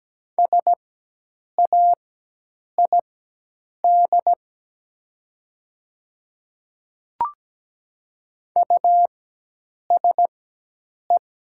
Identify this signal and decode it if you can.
Morse code, heard as a keyed tone, about 17 wpm, spaced as for 8 wpm, 700 Hz. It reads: SAID USE
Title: 4x Character Spacing - Top 100 Words - 17wpm